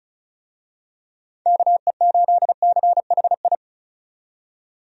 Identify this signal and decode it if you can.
Morse code, heard as a keyed tone, about 36 wpm, 700 Hz. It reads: KE8CHI
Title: USA Call Signs - 6 Characters - 35wpm